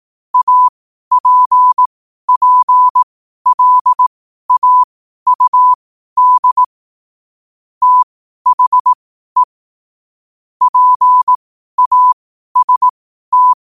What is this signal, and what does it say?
Morse code, heard as a keyed tone, about 18 wpm, spaced as for 12 wpm, 1000 Hz. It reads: APPLAUD THE PAST
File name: qotdfiles / 12wpm